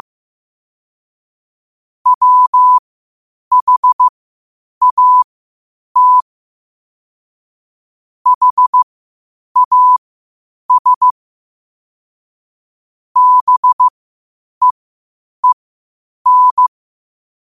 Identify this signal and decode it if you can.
Morse code, heard as a keyed tone, about 15 wpm, spaced as for 8 wpm, 1000 Hz. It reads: WHAT HAS BEEN
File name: qotdfiles / 8wpm